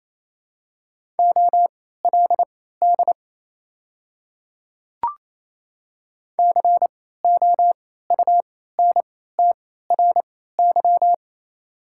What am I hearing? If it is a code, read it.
Morse code, heard as a keyed tone, about 27 wpm, spaced as for 16 wpm, 690 Hz. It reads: OLD COUNTRY
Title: 3x Character Spacing - Top 300 Words - 28wpm